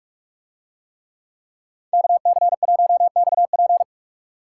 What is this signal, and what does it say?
Morse code, heard as a keyed tone, about 41 wpm, 700 Hz. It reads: KC1XP